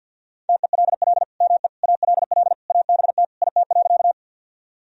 Morse code, heard as a keyed tone, about 50 wpm, 700 Hz. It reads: TELL ME ALL ABT IT.
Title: Sentences from Top 200 Words and Abbreviations 50wpm